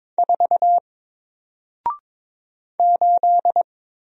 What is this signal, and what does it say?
Morse code, heard as a keyed tone, about 22 wpm, 700 Hz. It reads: 4 8
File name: Single-Letter-Number-22wpm